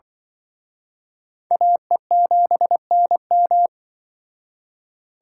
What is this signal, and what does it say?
Morse code, heard as a keyed tone, about 24 wpm, 700 Hz. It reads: AE7NM
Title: Speed Racing - USA Call Signs 20wpm